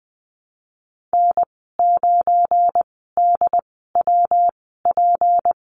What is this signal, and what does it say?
Morse code, heard as a keyed tone, about 20 wpm, spaced as for 15 wpm, 700 Hz. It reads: N9DWP
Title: 173 Lesson 28g - Number 9 - Call Signs 20wpm